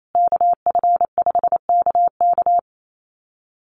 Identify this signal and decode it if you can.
Morse code, heard as a keyed tone, about 28 wpm, 700 Hz. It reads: KF5KK
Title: USA Call Signs Repeated 3x 28wpm